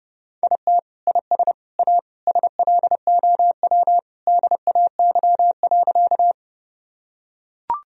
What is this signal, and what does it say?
Morse code, heard as a keyed tone, about 30 wpm, 700 Hz. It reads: IT IS A SLOW DAY.
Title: Sentences from Top 1000 Words - Review Part 2 - 30wpm